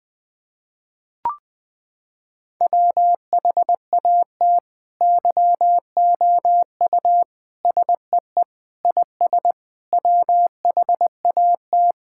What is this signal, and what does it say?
Morse code, heard as a keyed tone, about 20 wpm, 700 Hz. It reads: WHAT YOU SEE IS WHAT
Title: Sentences with 7-Word Phrases - 20wpm